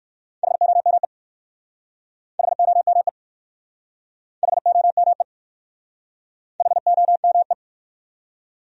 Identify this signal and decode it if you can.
Morse code, heard as a keyed tone, about 59 wpm, 700 Hz. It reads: HOME HOME HOME HOME